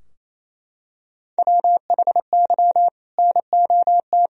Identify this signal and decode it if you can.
Morse code, heard as a keyed tone, about 28 wpm, 700 Hz. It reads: WHY NOT
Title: Sentences from Top 1000 Words - Review Part 3 - 28wpm